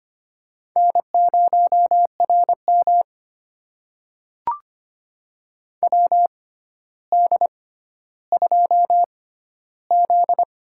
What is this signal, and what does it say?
Morse code, heard as a keyed tone, about 25 wpm, spaced as for 13 wpm, 700 Hz. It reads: N0RM WD2Z
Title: USA Call Signs - 4 Characters - 6x Character Spacing - 25wpm